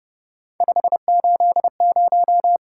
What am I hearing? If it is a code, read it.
Morse code, heard as a keyed tone, about 29 wpm, 700 Hz. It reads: 580